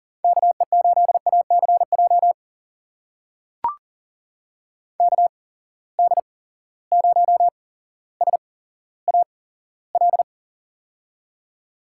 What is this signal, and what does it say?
Morse code, heard as a keyed tone, about 39 wpm, spaced as for 17 wpm, 700 Hz. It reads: KE8ACJ KD0SAL
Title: USA Call Signs - 6 Characters - 8x Character Spacing - 40wpm